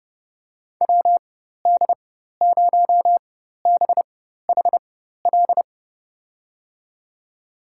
Morse code, heard as a keyed tone, about 30 wpm, spaced as for 14 wpm, 700 Hz. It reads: WD0BHL
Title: USA Call Signs - 6 Characters - 4x Character Spacing - 30wpm